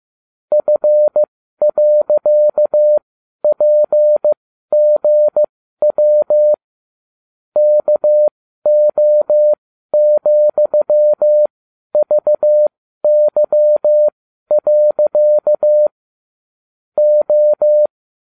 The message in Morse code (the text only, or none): F.PGW KO,VY. O